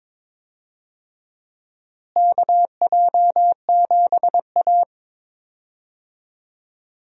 Morse code, heard as a keyed tone, about 22 wpm, 700 Hz. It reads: KJ7A
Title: USA Call Signs - 4 Characters - Repeated 3x - 22wpm